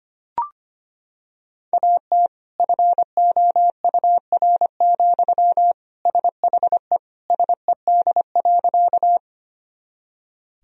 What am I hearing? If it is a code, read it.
Morse code, heard as a keyed tone, about 25 wpm, 700 Hz. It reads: AT FOUR, SHE SED.
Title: Sentences from Top 300 Words and Abbreviations 25wpm